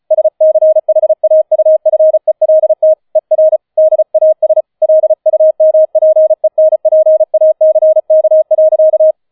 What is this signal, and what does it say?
Morse code, heard as a keyed tone, about 35 wpm, 630 Hz. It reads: SCHAUFELT ER DAS LUMPENPACK.